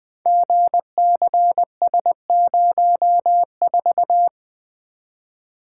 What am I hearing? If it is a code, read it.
Morse code, heard as a keyed tone, about 20 wpm, 700 Hz. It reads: GCS04
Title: Code Groups - 5 Letter:Numbers - 20wpm